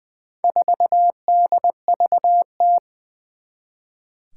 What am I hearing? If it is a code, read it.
Morse code, heard as a keyed tone, about 20 wpm, 700 Hz. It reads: 4DVT
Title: Four Challenging Letter-Number Combinations 20wpm